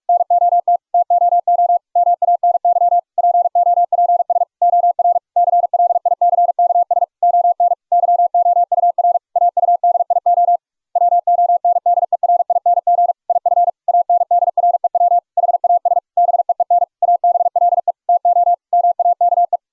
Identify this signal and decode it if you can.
Morse code, heard as a keyed tone, about 44 wpm, 690 Hz. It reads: NOT TOO MANY POPS OR CLICKS ON YOUR AUDIO WONDERING IF ANDREW HAS BEEN ABLE TO MAKE